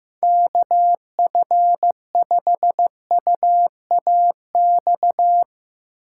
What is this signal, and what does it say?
Morse code, heard as a keyed tone, about 16 wpm, 700 Hz. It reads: KF5UAX
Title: Speed Racing - USA 2x3 Call Signs 15wpm